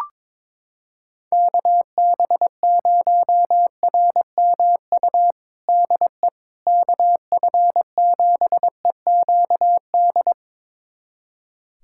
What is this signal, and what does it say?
Morse code, heard as a keyed tone, about 22 wpm, 700 Hz. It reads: KB0RMU DE KF7EQD
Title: USA Callsign DE Callsign - 6 Characters - 22wpm